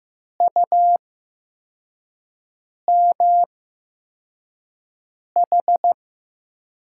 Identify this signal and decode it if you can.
Morse code, heard as a keyed tone, about 15 wpm, spaced as for 4 wpm, 700 Hz. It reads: UMH
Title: USA Call Signs - 8x Character Spacing - 15wpm